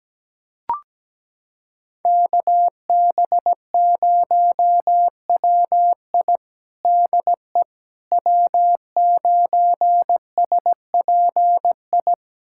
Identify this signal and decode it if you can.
Morse code, heard as a keyed tone, about 17 wpm, 700 Hz. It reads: KB0WI DE W9SPI
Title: USA Callsign DE Callsign - 5 Characters - 17wpm